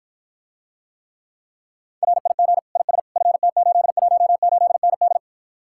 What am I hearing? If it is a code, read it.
Morse code, heard as a keyed tone, about 53 wpm, 700 Hz. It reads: RIG IS FT818ND